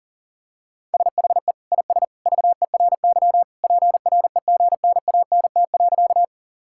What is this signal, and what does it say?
Morse code, heard as a keyed tone, about 40 wpm, 700 Hz. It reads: SHE IS VERY PREGNANT.
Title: Sentences from Top 1000 Words - Review Part 6 - 40wpm